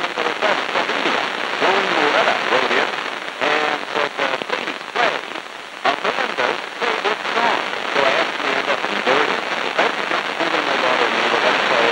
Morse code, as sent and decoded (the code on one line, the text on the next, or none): none